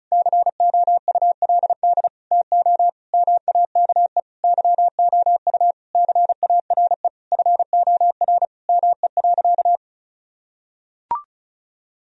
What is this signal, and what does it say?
Morse code, heard as a keyed tone, about 36 wpm, 700 Hz. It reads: COULD TO MAKE YOU CARE FOR ME.
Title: Sentences from Top 1000 Words - Review Part 1 - 35wpm